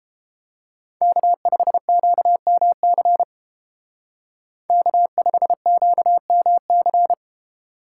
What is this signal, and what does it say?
Morse code, heard as a keyed tone, about 31 wpm, 700 Hz. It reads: K5QMC K5QMC